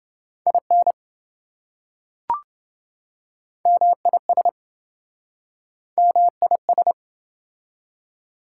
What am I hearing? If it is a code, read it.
Morse code, heard as a keyed tone, about 26 wpm, 700 Hz. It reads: IN MIS MIS